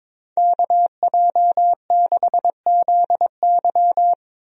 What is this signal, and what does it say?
Morse code, heard as a keyed tone, about 22 wpm, 700 Hz. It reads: KJ6ZY
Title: USA Call Signs - 5 Characters - 22wpm